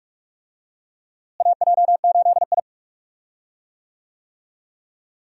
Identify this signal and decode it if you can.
Morse code, heard as a keyed tone, about 46 wpm, 700 Hz. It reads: AJ8I
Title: USA Call Signs - 4 Characters - Repeated 2x - 45wpm